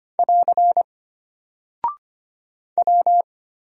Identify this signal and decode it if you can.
Morse code, heard as a keyed tone, about 25 wpm, 700 Hz. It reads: <AR> W
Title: Single Character (Letter:Number:Punctuation:Prosign) Alt 25wpm